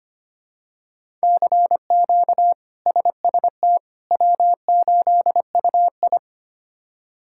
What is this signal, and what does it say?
Morse code, heard as a keyed tone, about 25 wpm, 700 Hz. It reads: CQ SST W8UI
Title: K1USN SST QSO Practice - 25wpm